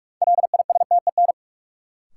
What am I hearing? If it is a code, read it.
Morse code, heard as a keyed tone, about 42 wpm, 700 Hz. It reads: LISTEN
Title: Sentences from Top 400 Words - New Word Drill - 45wpm